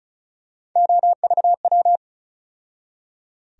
Morse code, heard as a keyed tone, about 34 wpm, 700 Hz. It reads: OVW
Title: USA Call Sign Postfixes - 3 Letters 35wpm